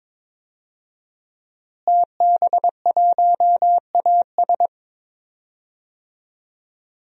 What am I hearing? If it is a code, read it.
Morse code, heard as a keyed tone, about 22 wpm, 700 Hz. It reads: TB1AS
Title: DXCC-100 Call Signs - 5 Characters - Repeated 2x - 22wpm